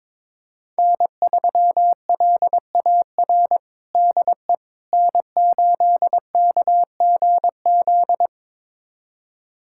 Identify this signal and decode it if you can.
Morse code, heard as a keyed tone, about 22 wpm, 700 Hz. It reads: N3LAR DE N8KGZ
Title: USA Callsign DE Callsign - 5 Characters - 22wpm